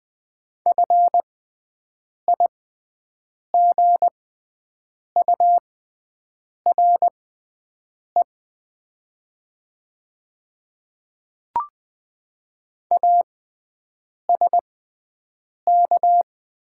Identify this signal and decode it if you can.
Morse code, heard as a keyed tone, about 20 wpm, spaced as for 7 wpm, 700 Hz. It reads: FIGURE ASK
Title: 6x Character Spacing - Top 500 Words - 20wpm